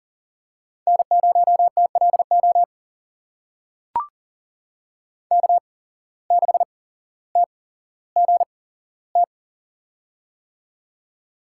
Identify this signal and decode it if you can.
Morse code, heard as a keyed tone, about 40 wpm, spaced as for 17 wpm, 700 Hz. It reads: N0TLO K6TGT
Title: USA Call Signs - 5 Characters - 8x Character Spacing - 40wpm